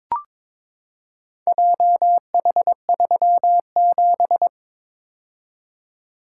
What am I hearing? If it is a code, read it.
Morse code, heard as a keyed tone, about 22 wpm, 700 Hz. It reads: JH37
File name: practice / 22wpm / Four Letter-Number Combinations - 22wpm